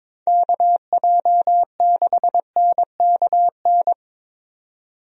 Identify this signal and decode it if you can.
Morse code, heard as a keyed tone, about 22 wpm, 700 Hz. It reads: KJ6NKN
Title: USA Call Signs - 6 Characters - 22wpm